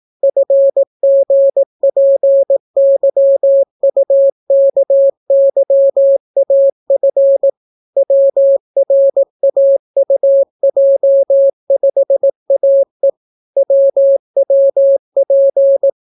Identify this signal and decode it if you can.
Morse code, heard as a keyed tone, about 18 wpm, 550 Hz. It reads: FGPYUKYAF WRAUJ5AE WWP